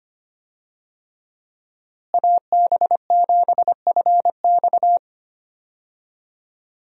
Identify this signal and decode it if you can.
Morse code, heard as a keyed tone, about 25 wpm, 700 Hz. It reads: AB7FX
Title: USA Call Signs - 5 Characters - Repeated 3x - 25wpm